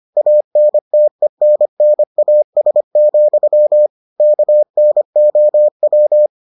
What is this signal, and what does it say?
Morse code, heard as a keyed tone, about 25 wpm, 600 Hz. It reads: ANTENNAS, KNOW